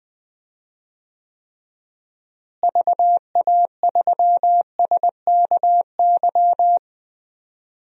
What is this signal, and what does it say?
Morse code, heard as a keyed tone, about 20 wpm, 700 Hz. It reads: VA3SKY
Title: DXCC-100 Call Signs - 6 Characters - Repeated 3x - 20wpm